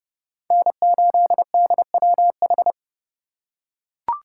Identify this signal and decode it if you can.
Morse code, heard as a keyed tone, about 31 wpm, 700 Hz. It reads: N8DWH
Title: USA Call Signs - 5 Characters - 4x Character Spacing - 30wpm